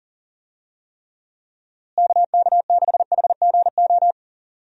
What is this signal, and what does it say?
Morse code, heard as a keyed tone, about 40 wpm, 700 Hz. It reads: KK6HGO
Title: USA Call Signs - 6 Characters - Repeated 2x - 40wpm